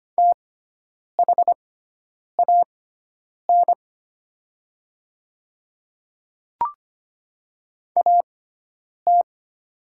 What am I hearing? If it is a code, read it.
Morse code, heard as a keyed tone, about 25 wpm, spaced as for 9 wpm, 700 Hz. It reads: THAN AT
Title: 6x Character Spacing - Top 100 Words - 25wpm